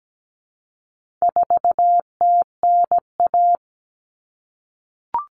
Morse code, heard as a keyed tone, about 17 wpm, 700 Hz. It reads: 4TNA
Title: Common Cut Numbers - 4-Digit 17wpm